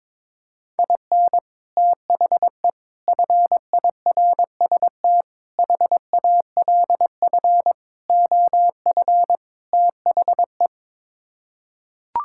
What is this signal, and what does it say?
Morse code, heard as a keyed tone, about 22 wpm, 700 Hz. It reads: IN THE FIRST HALF OF THE
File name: practice / 22wpm / Sets of 6 Words - 22wpm